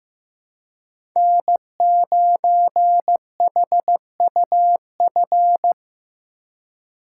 Morse code, heard as a keyed tone, about 15 wpm, 700 Hz. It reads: N9HUF
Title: USA Call Signs - 3x Character Spacing - 15wpm